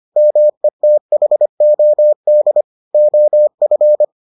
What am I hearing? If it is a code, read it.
Morse code, heard as a keyed tone, about 25 wpm, 600 Hz. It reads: METHOD OF